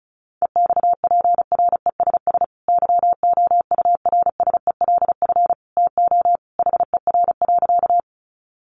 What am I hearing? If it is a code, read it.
Morse code, heard as a keyed tone, about 34 wpm, 700 Hz. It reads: EXPRESS YOURSELF TO HER.
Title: Sentences from Top 1100 Words 35wpm